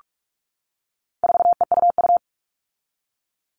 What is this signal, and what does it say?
Morse code, heard as a keyed tone, about 44 wpm, 700 Hz. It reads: 4EUU
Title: All Cut Numbers - 4-Digit 45wpm